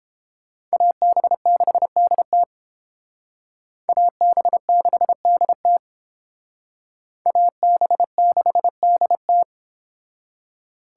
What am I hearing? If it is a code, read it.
Morse code, heard as a keyed tone, about 30 wpm, 700 Hz. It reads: AB6DT AB6DT AB6DT